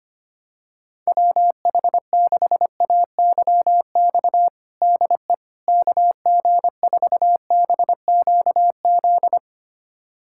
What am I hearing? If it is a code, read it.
Morse code, heard as a keyed tone, about 25 wpm, 700 Hz. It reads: WH6AYX DE KG4BQZ
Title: USA Callsign DE Callsign 25wpm